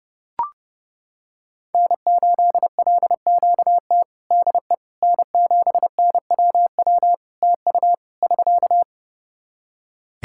Morse code, heard as a keyed tone, about 31 wpm, 700 Hz. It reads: N8LQT DE N7NWW TU <SK>
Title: Practice QSO - Exchange 4 - Ending - 30wpm